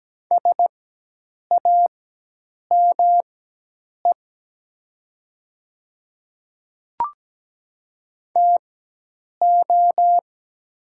Morse code, heard as a keyed tone, about 17 wpm, spaced as for 8 wpm, 700 Hz. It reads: SAME TO
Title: 4x Character Spacing - Top 200 Words - 17wpm